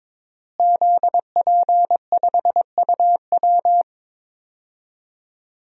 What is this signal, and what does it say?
Morse code, heard as a keyed tone, about 21 wpm, 700 Hz. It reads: ZP5UW
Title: DXCC-100 Call Signs - 5 Characters - Repeated 2x - 22wpm